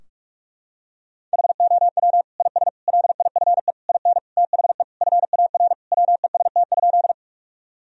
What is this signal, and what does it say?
Morse code, heard as a keyed tone, about 41 wpm, 700 Hz. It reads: HOW IS LIFE IN THE FAR WEST?
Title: Sentences from Top 1000 Words - Review Part 2 - 45wpm